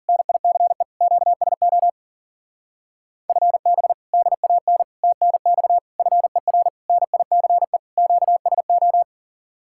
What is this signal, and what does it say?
Morse code, heard as a keyed tone, about 49 wpm, 700 Hz. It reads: NICE QSO FB DAN TNX FER NICE QSO
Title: Speed-Racing Common QSO Phrases - 40wpm